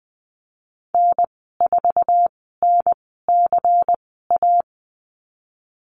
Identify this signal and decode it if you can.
Morse code, heard as a keyed tone, about 19 wpm, spaced as for 14 wpm, 700 Hz. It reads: N4NCA
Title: 193 Lesson 31g - Letter v - Call Signs 20wpm